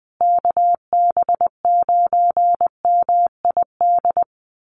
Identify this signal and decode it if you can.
Morse code, heard as a keyed tone, about 20 wpm, 700 Hz. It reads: KB9MID
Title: USA Call Signs Repeated 2x 20wpm